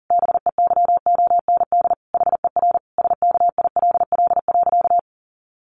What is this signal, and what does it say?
Morse code, heard as a keyed tone, about 40 wpm, 700 Hz. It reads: BEYOND HER SKILL.
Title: Sentences from Top 1200 Words 40wpm